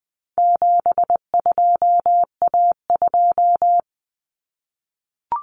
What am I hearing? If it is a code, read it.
Morse code, heard as a keyed tone, about 20 wpm, 700 Hz. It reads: 72A2
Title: Common Cut Numbers - 4-Digit 20wpm